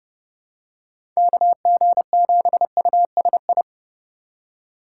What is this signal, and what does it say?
Morse code, heard as a keyed tone, about 32 wpm, 700 Hz. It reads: KG7USI
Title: USA Call Signs - 6 Characters - 2x Character Spacing - 30wpm